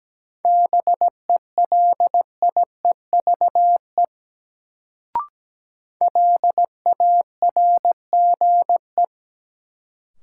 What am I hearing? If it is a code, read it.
Morse code, heard as a keyed tone, about 17 wpm, 700 Hz. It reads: BELIEVE LARGE
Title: Sets of 4 Words - New Word Drill - 17wpm